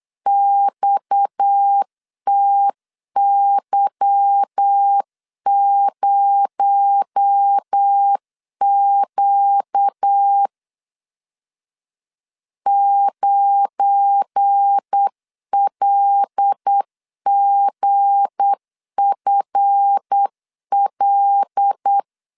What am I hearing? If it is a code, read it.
Morse code, heard as a keyed tone, about 8 wpm, 800 Hz. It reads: XTY0Q 9LGFL
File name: LEZIONI / 9 286 8wpm 3425